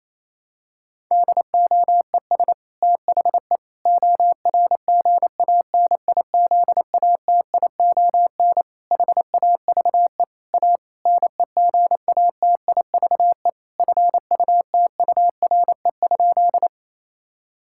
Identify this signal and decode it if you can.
Morse code, heard as a keyed tone, about 27 wpm, 700 Hz. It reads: DOES THE ORGANIZATION HAVE A NEGATIVE FUTURE?
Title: Sentences from Top 1000 Words - Review Part 6 - 28wpm